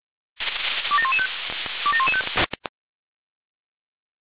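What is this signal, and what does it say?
Five-tone selective call (ZVEI), heard as quick runs of five tones, tones of about 70 ms. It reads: 27115 27115
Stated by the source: Five-tone ZVEI selective call received by SDR.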